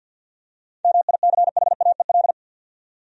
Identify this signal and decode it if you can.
Morse code, heard as a keyed tone, about 52 wpm, 700 Hz. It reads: MICHAEL
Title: USA Common Names - 50wpm